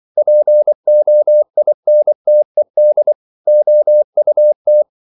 Morse code, heard as a keyed tone, about 24 wpm, 600 Hz. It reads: POINTED OUT